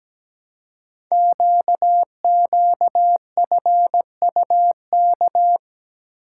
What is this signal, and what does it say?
Morse code, heard as a keyed tone, about 17 wpm, 700 Hz. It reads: QQFUK
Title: Code Groups - 5 Letters - 17wpm